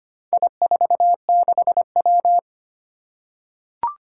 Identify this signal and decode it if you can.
Morse code, heard as a keyed tone, about 25 wpm, 700 Hz. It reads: I46W
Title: Four Challenging Letter-Number Combinations 25wpm